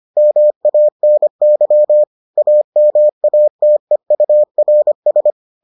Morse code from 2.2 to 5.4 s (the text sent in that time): AMATEURS